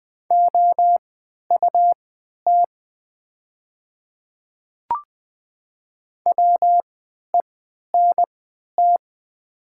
Morse code, heard as a keyed tone, about 20 wpm, spaced as for 11 wpm, 700 Hz. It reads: OUT WENT
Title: 3x Character Spacing - Top 200 Words - 20wpm